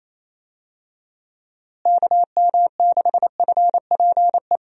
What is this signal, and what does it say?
Morse code, heard as a keyed tone, about 28 wpm, 700 Hz. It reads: KM6FPE
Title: USA Call Signs - 6 Characters - Repeated 2x - 28wpm